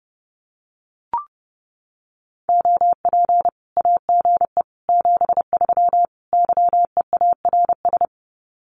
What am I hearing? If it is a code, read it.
Morse code, heard as a keyed tone, about 30 wpm, 700 Hz. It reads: OP AGE 73 YEARS
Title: Common QSO Phrases 30wpm